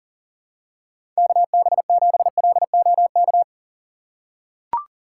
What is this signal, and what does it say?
Morse code, heard as a keyed tone, about 40 wpm, 700 Hz. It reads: KB7LOK
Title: USA Call Signs - 6 Characters - 3x Character Spacing - 40wpm